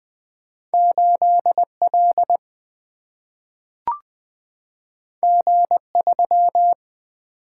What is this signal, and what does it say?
Morse code, heard as a keyed tone, about 20 wpm, 700 Hz. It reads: 8L G3
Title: Single-Letter Single-Number 20wpm